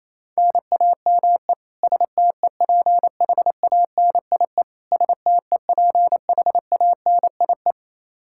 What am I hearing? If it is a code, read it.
Morse code, heard as a keyed tone, about 28 wpm, 700 Hz. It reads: NAME STEPHANIE STEPHANIE
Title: Practice QSO - Exchange 1 - Beginning - 28wpm